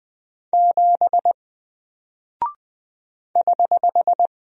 Morse code, heard as a keyed tone, about 20 wpm, 700 Hz. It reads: 7 <HH>
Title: Single Character (Letter:Number:Punctuation:Prosign) Alt 20wpm